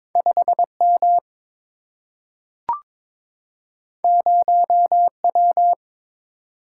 Morse code, heard as a keyed tone, about 22 wpm, 700 Hz. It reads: 5M 0W
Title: Single-Letter Single-Number 22wpm